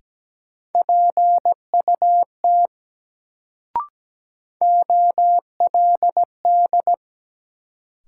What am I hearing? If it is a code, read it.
Morse code, heard as a keyed tone, about 17 wpm, 700 Hz. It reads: PUT OLD
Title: Sentences from Top 200 Words - New Word Drill - 17wpm